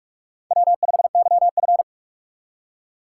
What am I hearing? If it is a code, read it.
Morse code, heard as a keyed tone, about 41 wpm, 700 Hz. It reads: W5YF